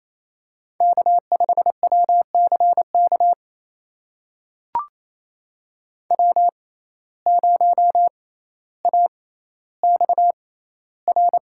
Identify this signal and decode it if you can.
Morse code, heard as a keyed tone, about 28 wpm, spaced as for 14 wpm, 700 Hz. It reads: K5WCK W0AXR
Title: USA Call Signs - 6x Character Spacing - 28wpm